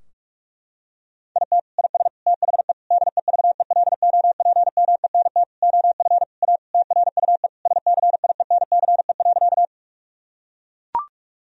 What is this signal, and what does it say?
Morse code, heard as a keyed tone, about 42 wpm, 700 Hz. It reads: IT IS THE DEVELOPMENT OF A TRUE SCIENCE.
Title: Sentences from Top 1000 Words - Review Part 4 - 45wpm